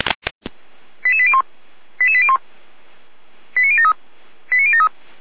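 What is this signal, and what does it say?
Five-tone selective call (ZVEI), heard as quick runs of five tones, tones of about 70 ms. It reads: 88801 88801 89073 89073
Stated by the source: ZVEI five-tone call 89014 received on SDR.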